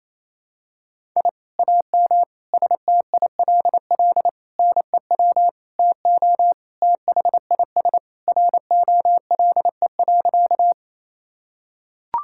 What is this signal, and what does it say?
Morse code, heard as a keyed tone, about 27 wpm, 700 Hz. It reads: I AM STILL NEW TO THIS ROLE.
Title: Sentences from Top 1000 Words - Review Part 6 - 28wpm